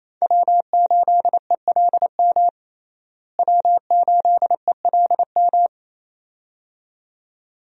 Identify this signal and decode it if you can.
Morse code, heard as a keyed tone, about 28 wpm, 700 Hz. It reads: W8ELM W8ELM
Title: USA Call Signs - 5 Characters - Repeated 2x - 28wpm